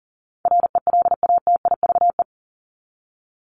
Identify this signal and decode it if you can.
Morse code, heard as a keyed tone, about 40 wpm, 700 Hz. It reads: RELATIVE
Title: Top 1251-1500 Words 40wpm